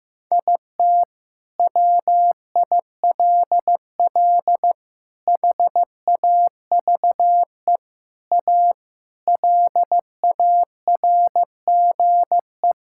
Morse code, heard as a keyed tone, about 15 wpm, 700 Hz. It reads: IT WILL HAVE A LARGE
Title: Sentences from Top 1000 Words - Review Part 6 - 15wpm